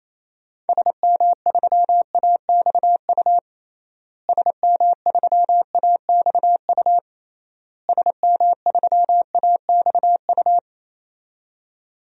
Morse code, heard as a keyed tone, about 28 wpm, 700 Hz. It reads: SM3AXU SM3AXU SM3AXU
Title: DXCC-100 Call Signs - 6 Characters - Repeated 3x - 28wpm